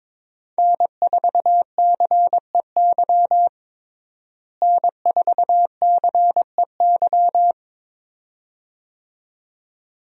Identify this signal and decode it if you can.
Morse code, heard as a keyed tone, about 22 wpm, 700 Hz. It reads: N4CEY N4CEY
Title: USA Call Signs - 5 Characters - Repeated 2x - 22wpm